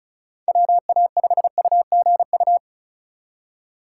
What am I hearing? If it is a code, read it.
Morse code, heard as a keyed tone, about 43 wpm, 700 Hz. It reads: WA5UGU
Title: Speed Racing - USA 2x3 Call Signs 35wpm